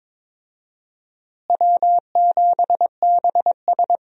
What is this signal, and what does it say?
Morse code, heard as a keyed tone, about 22 wpm, 700 Hz. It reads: W7BS
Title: USA Call Signs - 4 Characters - Repeated 2x - 22wpm